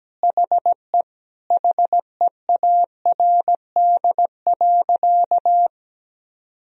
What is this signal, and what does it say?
Morse code, heard as a keyed tone, about 17 wpm, 700 Hz. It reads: HE HEARD.
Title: Sentences from Top 1000 Words - Review Part 1 - 17wpm